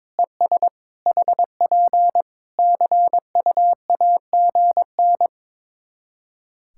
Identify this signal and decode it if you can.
Morse code, heard as a keyed tone, about 22 wpm, 700 Hz. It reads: ES HP CUAGN
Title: Practice QSO - Exchange 4 - Ending - 22wpm